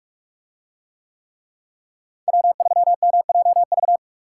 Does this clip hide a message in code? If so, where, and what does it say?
Yes: Morse code from 2.3 to 4.0 s, heard as a keyed tone, about 44 wpm, 700 Hz. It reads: W3MJV